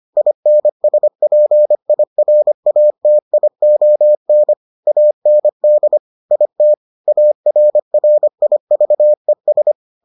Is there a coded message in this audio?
Yes: Morse code from 0.2 to 9.7 s, heard as a keyed tone, about 25 wpm, 600 Hz. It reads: INSPIRATION AND IT ARRIVES